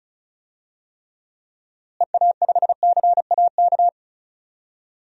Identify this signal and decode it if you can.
Morse code, heard as a keyed tone, about 34 wpm, 700 Hz. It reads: EA5CAK
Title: DXCC-100 Call Signs - 6 Characters - Repeated 3x - 35wpm